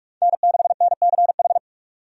Morse code, heard as a keyed tone, about 46 wpm, 700 Hz. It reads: N6NCH
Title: USA Call Signs - 5 Characters - 45wpm